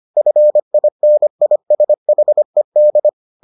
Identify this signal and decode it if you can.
Morse code, heard as a keyed tone, about 25 wpm, 600 Hz. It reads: FINISHED